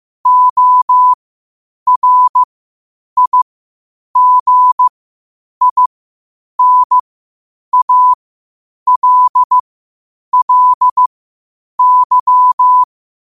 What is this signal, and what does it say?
Morse code, heard as a keyed tone, about 15 wpm, spaced as for 8 wpm, 1000 Hz. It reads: ORIGINALLY
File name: qotdfiles / 8wpm